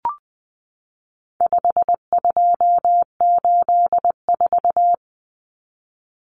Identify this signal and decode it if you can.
Morse code, heard as a keyed tone, about 20 wpm, 700 Hz. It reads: 5284
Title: Common Cut Numbers - 4-Digit 20wpm